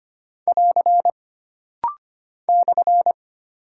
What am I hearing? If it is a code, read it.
Morse code, heard as a keyed tone, about 25 wpm, 700 Hz. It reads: <AR> /
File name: Single Punctuation-Prosign - Rapid-Fire - 25wpm